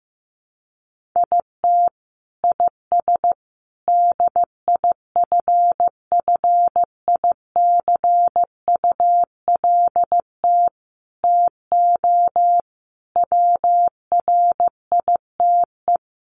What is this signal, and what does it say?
Morse code, heard as a keyed tone, about 15 wpm, 700 Hz. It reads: IT IS DIFFICULT TO WRITE
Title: Sentences from Top 1500 Words 15wpm